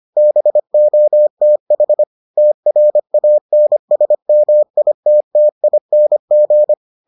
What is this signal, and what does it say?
Morse code, heard as a keyed tone, about 25 wpm, 600 Hz. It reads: BOTH TRANSMITTING